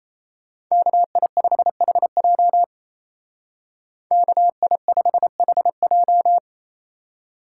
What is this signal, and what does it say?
Morse code, heard as a keyed tone, about 35 wpm, 700 Hz. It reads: KI5HJ KI5HJ